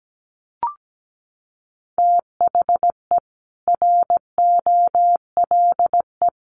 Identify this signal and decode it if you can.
Morse code, heard as a keyed tone, about 17 wpm, 700 Hz. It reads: THE ROLE
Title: Sets of 4 Words 17wpm